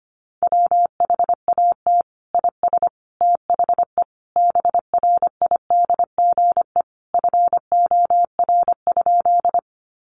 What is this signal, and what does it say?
Morse code, heard as a keyed tone, about 25 wpm, 700 Hz. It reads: WHAT IS THE BRIDGE FOR?
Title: Sentences from Top 1200 Words 25wpm